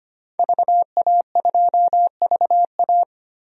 Morse code, heard as a keyed tone, about 25 wpm, 700 Hz. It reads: VA2VA
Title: DXCC-100 Call Signs - 5 Characters - 25wpm